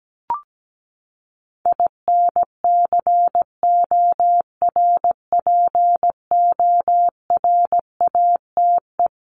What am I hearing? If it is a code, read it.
Morse code, heard as a keyed tone, about 17 wpm, 700 Hz. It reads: INCORPORATE
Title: Top 1751-2000 Words 17wpm